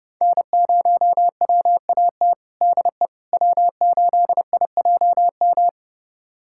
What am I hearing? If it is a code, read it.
Morse code, heard as a keyed tone, about 31 wpm, 700 Hz. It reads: N0WAT DE W8IJM
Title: USA Callsign DE Callsign - 5 Characters - 30wpm